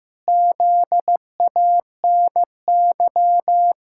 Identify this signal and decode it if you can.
Morse code, heard as a keyed tone, about 15 wpm, 700 Hz. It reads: ZANY
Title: Pangram Challenge v2 - 15wpm